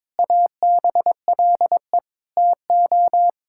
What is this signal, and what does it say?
Morse code, heard as a keyed tone, about 22 wpm, 700 Hz. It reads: ABLE TO
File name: practice / 22wpm / Sets of 6 Words - 22wpm